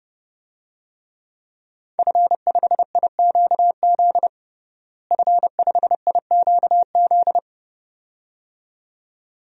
Morse code, heard as a keyed tone, about 32 wpm, 700 Hz. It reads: F5IQZ F5IQZ
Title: DXCC-100 Call Signs - 5 Characters - Repeated 2x - 30wpm